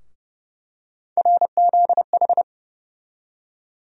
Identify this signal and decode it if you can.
Morse code, heard as a keyed tone, about 30 wpm, 700 Hz. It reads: RZH
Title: USA Call Sign Postfixes - 3 Letters 30wpm